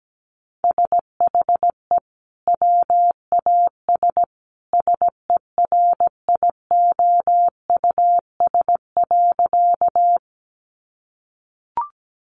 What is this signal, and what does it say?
Morse code, heard as a keyed tone, about 17 wpm, 700 Hz. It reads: SHE WAS SERIOUS.
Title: Sentences from Top 1100 Words 17wpm